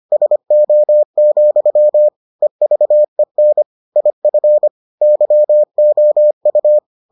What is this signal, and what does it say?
Morse code, heard as a keyed tone, about 25 wpm, 600 Hz. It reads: SO, EVEN IF YOU